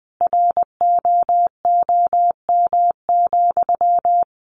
Morse code, heard as a keyed tone, about 20 wpm, 700 Hz. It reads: ROOM,
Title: Sentences from Top 1400 Words 20wpm